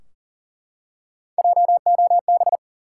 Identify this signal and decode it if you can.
Morse code, heard as a keyed tone, about 40 wpm, 700 Hz. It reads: JOB
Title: Three Letter Words - 40wpm